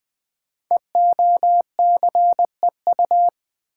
Morse code, heard as a keyed tone, about 20 wpm, 700 Hz. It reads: EOCEU
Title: Code Groups - 5 Letter:Numbers - 20wpm